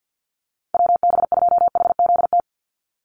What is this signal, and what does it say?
Morse code, heard as a keyed tone, about 50 wpm, 700 Hz. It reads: WB2HZT
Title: USA Call Signs 50wpm